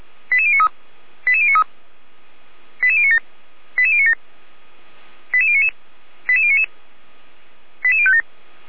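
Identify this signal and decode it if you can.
Five-tone selective call (ZVEI), heard as quick runs of five tones, tones of about 70 ms. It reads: 88093 88093 88097 88097 88099 88099 88057
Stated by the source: ZVEI five-tone selective call received on SDR, archived by fire brigade Feuerwehr Orlamünde (f85360).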